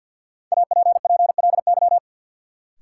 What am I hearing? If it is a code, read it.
Morse code, heard as a keyed tone, about 52 wpm, 700 Hz. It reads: APPLY
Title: Sentences from Top 1200 Words - New Word Drill Alt - 50wpm